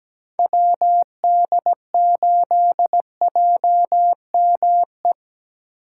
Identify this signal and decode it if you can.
Morse code, heard as a keyed tone, about 20 wpm, 700 Hz. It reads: WD8JME
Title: Speed Racing - USA 2x3 Call Signs 17wpm